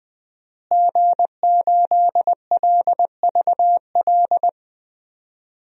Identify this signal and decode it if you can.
Morse code, heard as a keyed tone, about 20 wpm, 700 Hz. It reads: G8LVL
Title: DXCC-100 Call Signs - 5 Characters - Repeated 3x - 20wpm